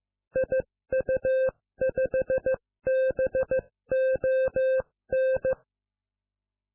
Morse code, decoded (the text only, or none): IU5BON